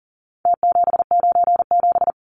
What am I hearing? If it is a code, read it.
Morse code, heard as a keyed tone, about 40 wpm, 700 Hz. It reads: T797
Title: Numbers - Contest Style - 4-Digit 40wpm